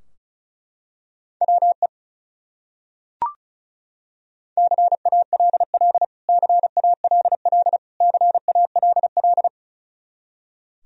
Morse code, heard as a keyed tone, about 34 wpm, 700 Hz. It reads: WE CALL CALL CALL
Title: Top 100 Words Repeated 35wpm